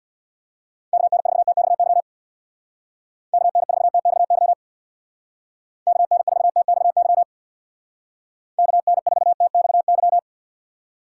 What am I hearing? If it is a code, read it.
Morse code, heard as a keyed tone, about 58 wpm, 700 Hz. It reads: KN4TXY KN4TXY KN4TXY KN4TXY
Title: Speed Racing - USA 2x3 Call Signs 50wpm